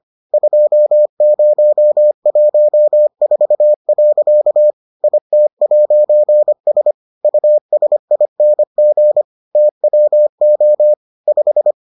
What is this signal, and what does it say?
Morse code, heard as a keyed tone, about 25 wpm, 600 Hz. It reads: 2014. IT'S USING TWO 5